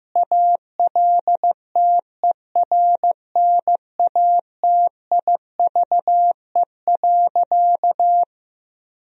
Morse code, heard as a keyed tone, about 16 wpm, 700 Hz. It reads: ALTERNATIVE.
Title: Sentences from Top 1000 Words - Review Part 4 - 15wpm